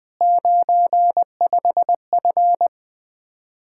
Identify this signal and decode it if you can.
Morse code, heard as a keyed tone, about 20 wpm, 700 Hz. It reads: 95F
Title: Speed-Racing Common QSO Phrases Encore - 20wpm